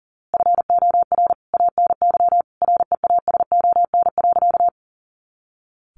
Morse code, heard as a keyed tone, about 40 wpm, 700 Hz. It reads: FOR ANY REASON.
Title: A Sound Of Thunder 40wpm